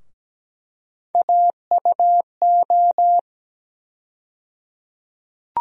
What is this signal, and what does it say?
Morse code, heard as a keyed tone, about 17 wpm, 700 Hz. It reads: AUO
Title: USA Call Sign Postfixes - 3 Letters 17wpm